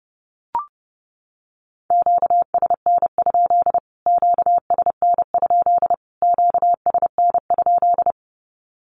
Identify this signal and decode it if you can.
Morse code, heard as a keyed tone, about 30 wpm, 700 Hz. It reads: QSN? QSN? QSN?
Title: Q-Codes Repeated 3x 30wpm